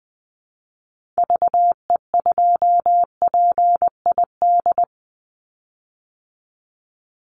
Morse code, heard as a keyed tone, about 20 wpm, 700 Hz. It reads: VE2PID
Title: Call Signs Repeated 3x - 20wpm